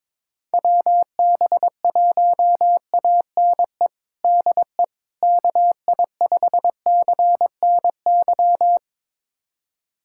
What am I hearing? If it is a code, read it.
Morse code, heard as a keyed tone, about 22 wpm, 700 Hz. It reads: WB1ANE DE KI5CNY
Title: USA Callsign DE Callsign - 6 Characters - 22wpm